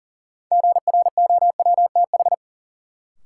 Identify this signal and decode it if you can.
Morse code, heard as a keyed tone, about 40 wpm, 700 Hz. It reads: GROWTH
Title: Sentences from Top 1300 Words - New Word Drill Alt - 40wpm